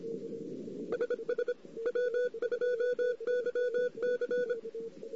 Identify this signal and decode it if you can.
Morse code, heard as a keyed tone, about 25 wpm, 500 Hz. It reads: SS W2YC
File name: CW / 2008-ss-cw-k5tr-op-n5rz -14:00